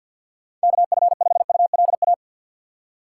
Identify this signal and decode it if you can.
Morse code, heard as a keyed tone, about 53 wpm, 700 Hz. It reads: KF5ULA
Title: USA Call Signs - 6 Characters - 6x Character Spacing - 50wpm